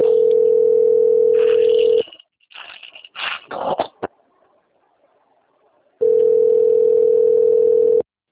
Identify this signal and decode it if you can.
Ringback tone (North America), 440 + 480 Hz.